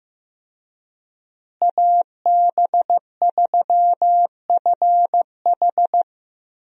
Morse code, heard as a keyed tone, about 15 wpm, 700 Hz. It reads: AB3FH
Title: USA Call Signs - 5 Characters - Repeated 2x - 15wpm